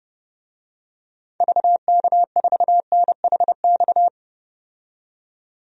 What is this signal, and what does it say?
Morse code, heard as a keyed tone, about 30 wpm, 700 Hz. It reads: VK4NHX
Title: DXCC-100 Call Signs - 6 Characters - 30wpm